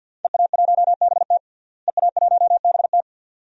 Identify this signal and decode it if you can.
Morse code, heard as a keyed tone, about 52 wpm, 700 Hz. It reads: EA1BT EA1BT
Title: DXCC-100 Call Signs - 5 Characters - Repeated 2x - 50wpm